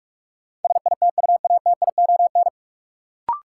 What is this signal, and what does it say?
Morse code, heard as a keyed tone, about 42 wpm, 700 Hz. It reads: SITUATION
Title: Sentences from Top 1200 Words - New Word Drill - 45wpm